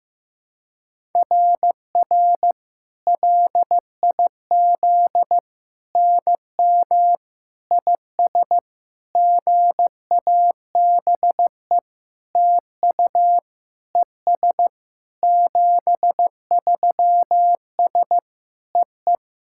Morse code, heard as a keyed tone, about 15 wpm, 700 Hz. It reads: RR LIZ NM IS GABE TU ES 73S EE
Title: SOTA QSO Practice 15wpm